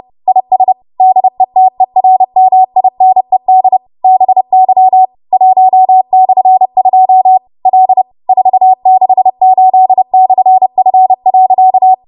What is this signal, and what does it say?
Morse code, heard as a keyed tone, about 30 wpm, 750 Hz. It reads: IS DETERMINED BY 1/2 L 468/F.
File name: Morse / 30WPM